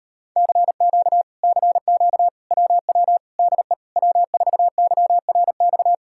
Morse code, heard as a keyed tone, about 39 wpm, 700 Hz. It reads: CQ CQ WW DE W4YRX